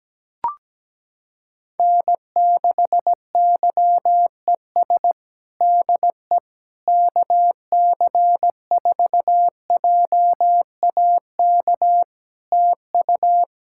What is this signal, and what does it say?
Morse code, heard as a keyed tone, about 17 wpm, 700 Hz. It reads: N6YES DE KC4JAK TU
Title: K1USN SST QSO Practice 17wpm